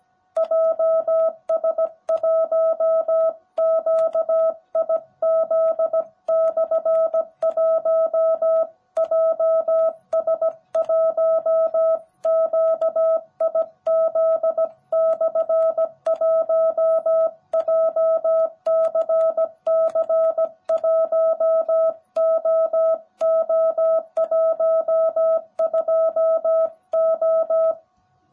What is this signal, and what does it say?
Morse code, heard as a keyed tone, about 17 wpm, 660 Hz. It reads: JS1QIZ/1 JS1QIZ/1 JCC1OO12O
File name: jh8cbh / 2021cw / cw18b